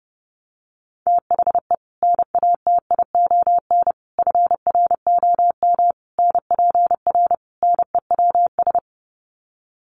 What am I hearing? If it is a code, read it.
Morse code, heard as a keyed tone, about 30 wpm, 700 Hz. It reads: THE NATION FROM NPR NEWS
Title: Sets of 5 Words 30wpm